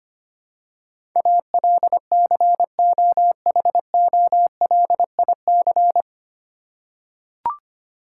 Morse code, heard as a keyed tone, about 25 wpm, 700 Hz. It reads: ALCOHOLIC
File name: practice / 25wpm / 9 Letter Words - First 3 Letters Given - 25wpm